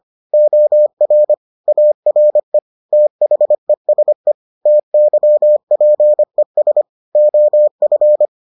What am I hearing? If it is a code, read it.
Morse code, heard as a keyed tone, about 25 wpm, 600 Hz. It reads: OR ARE THESE TYPES OF